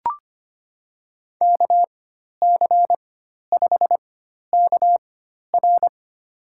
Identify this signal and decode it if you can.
Morse code, heard as a keyed tone, about 25 wpm, spaced as for 12 wpm, 700 Hz. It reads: KC5KR
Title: USA Call Signs - 4x Character Spacing - 25wpm